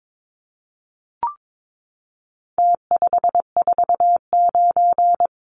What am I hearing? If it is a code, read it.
Morse code, heard as a keyed tone, about 22 wpm, 700 Hz. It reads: T549
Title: Numbers - Contest Style - 4-Digit 22wpm